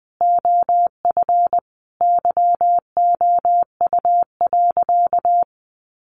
Morse code, heard as a keyed tone, about 19 wpm, 700 Hz. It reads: OF YOU.
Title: Sentences from Top 1400 Words 20wpm